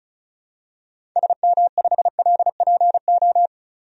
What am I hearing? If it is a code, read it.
Morse code, heard as a keyed tone, about 36 wpm, 700 Hz. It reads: SM5LPO
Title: DXCC-100 Call Signs - 6 Characters - 35wpm